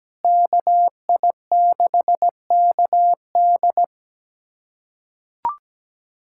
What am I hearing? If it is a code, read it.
Morse code, heard as a keyed tone, about 17 wpm, 700 Hz. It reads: KI6KD
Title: USA Callsign DE Callsign - 5 Characters - 17wpm